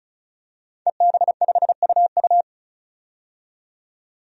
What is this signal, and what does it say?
Morse code, heard as a keyed tone, about 36 wpm, 700 Hz. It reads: EB5UU
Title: DXCC-100 Call Signs - 5 Characters - 35wpm